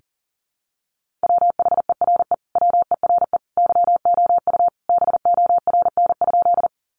Morse code, heard as a keyed tone, about 40 wpm, 700 Hz. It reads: WHERE WERE YOU BORN?
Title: Sentences from Top 1000 Words 40wpm